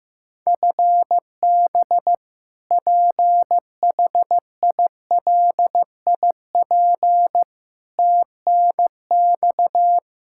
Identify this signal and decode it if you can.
Morse code, heard as a keyed tone, about 15 wpm, 700 Hz. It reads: FB PHILIP TNX
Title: Practice QSO - Exchange 3 - More Info - 15wpm